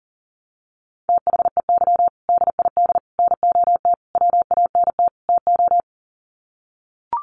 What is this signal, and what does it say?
Morse code, heard as a keyed tone, about 40 wpm, 700 Hz. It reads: THEY DID NOT WANT TO
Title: Sets of 5 Words 40wpm